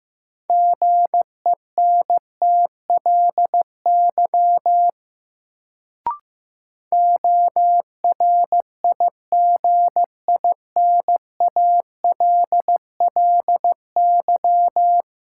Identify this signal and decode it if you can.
Morse code, heard as a keyed tone, about 15 wpm, 700 Hz. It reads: GENTLY ORIGINALLY
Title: Sentences from Top 1500 Words - New Word Drill - 15wpm